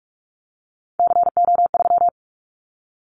Morse code, heard as a keyed tone, about 44 wpm, 700 Hz. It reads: CO3
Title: Callsign Prefixes for Top 100 DXCC - 45wpm